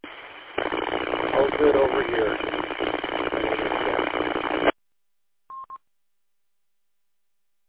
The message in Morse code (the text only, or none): none